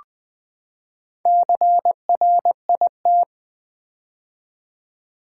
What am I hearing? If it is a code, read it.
Morse code, heard as a keyed tone, about 20 wpm, 700 Hz. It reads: CRIT
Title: Build A Word - 4 Letters At A Time - 10-Letter Words - 20wpm